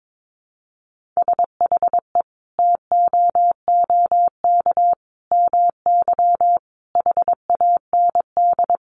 Morse code, heard as a keyed tone, about 22 wpm, 700 Hz. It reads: SHE TOOK MY HAND
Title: Sentences from Top 1500 Words 22wpm